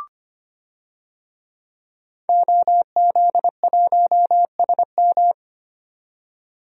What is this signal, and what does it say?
Morse code, heard as a keyed tone, about 25 wpm, 700 Hz. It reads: OZ1SM
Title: DXCC-100 Call Signs - 5 Characters - 25wpm